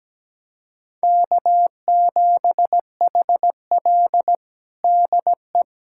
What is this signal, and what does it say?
Morse code, heard as a keyed tone, about 17 wpm, 700 Hz. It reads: K7HL DE
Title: USA Callsign DE Callsign - 4 Characters - 17wpm